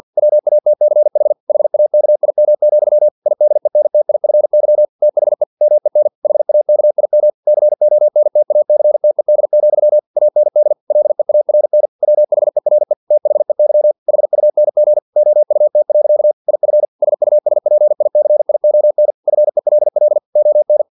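Morse code, heard as a keyed tone, about 50 wpm, 600 Hz. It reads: WATCH HAKIM, IDENTIFY THE MEN HAKIM CONTACTED, AND LEARN WHERE THEY HUNG OUT. IF SUSPICION FELL ON
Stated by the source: Morse practice text from an English-language novel at 49 wpm.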